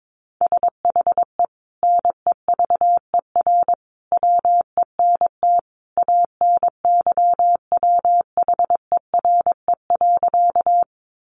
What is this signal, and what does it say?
Morse code, heard as a keyed tone, about 22 wpm, 700 Hz. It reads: SHE NEVER WENT ANYWHERE.
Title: Sentences from Top 1400 Words 22wpm